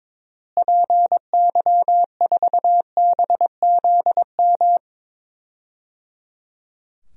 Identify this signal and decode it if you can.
Morse code, heard as a keyed tone, about 22 wpm, 700 Hz. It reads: PY4BZM
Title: DXCC-100 Call Signs - 6 Characters - Repeated 3x - 22wpm